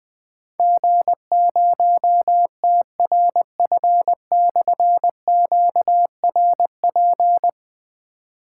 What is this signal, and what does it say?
Morse code, heard as a keyed tone, about 20 wpm, 700 Hz. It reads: G0TRF/QRP
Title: Difficult Call Signs 20wpm